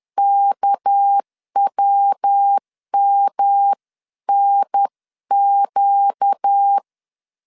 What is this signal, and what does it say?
Morse code, heard as a keyed tone, about 10 wpm, 800 Hz. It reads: KWMNQ
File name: LEZIONI / 6 PKX0 / 6 PK 10wpm 2436